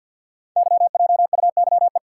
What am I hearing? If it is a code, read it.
Morse code, heard as a keyed tone, about 51 wpm, 690 Hz. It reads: YJUYE
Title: Code Groups - 5 Letters - 50wpm